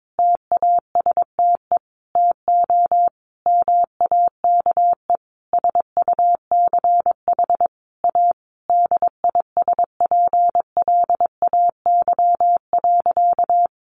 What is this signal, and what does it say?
Morse code, heard as a keyed tone, about 22 wpm, 700 Hz. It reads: TASTE TO MAKE SUCH A DISPLAY.
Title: Sentences from Top 1200 Words 22wpm